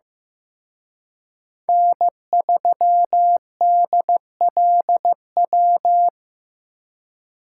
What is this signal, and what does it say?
Morse code, heard as a keyed tone, about 18 wpm, 700 Hz. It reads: N3DLW
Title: Speed Racing - USA 1x3 Call Signs 15wpm